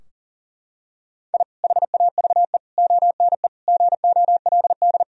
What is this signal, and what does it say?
Morse code, heard as a keyed tone, about 37 wpm, 700 Hz. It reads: I HAVE ONE GOLD
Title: Sentences from Top 1000 Words - Review Part 3 - 40wpm